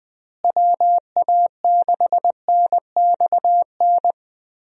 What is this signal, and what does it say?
Morse code, heard as a keyed tone, about 20 wpm, 700 Hz. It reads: WA6NXN
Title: USA Call Signs - 6 Characters - Repeated 2x - 20wpm